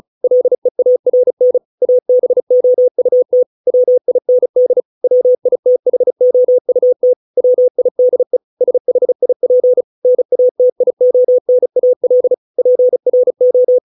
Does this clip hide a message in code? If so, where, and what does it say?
Yes: Morse code from 0.2 to 13.8 s, heard as a keyed tone, about 35 wpm, 500 Hz. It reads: LEARN ABOUT WIND WITHOUT WIDE SHIP NATIONAL PRO